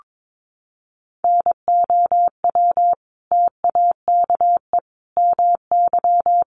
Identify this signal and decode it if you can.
Morse code, heard as a keyed tone, about 22 wpm, 700 Hz. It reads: NOW TAKE MY
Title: Sentences from Top 1100 Words 22wpm